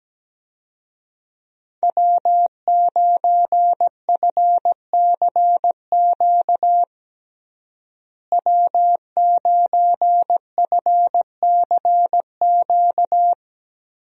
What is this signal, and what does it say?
Morse code, heard as a keyed tone, about 17 wpm, 700 Hz. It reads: W9FCQ W9FCQ
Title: USA Call Signs - 5 Characters - Repeated 3x - 17wpm